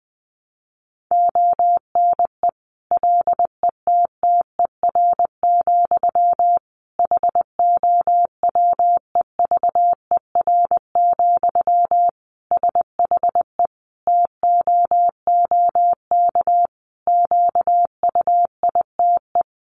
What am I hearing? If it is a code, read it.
Morse code, heard as a keyed tone, about 20 wpm, 700 Hz. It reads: ONE LETTER, HOWEVER, SHE TOOK QUITE
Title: Sentences from Top 1200 Words 20wpm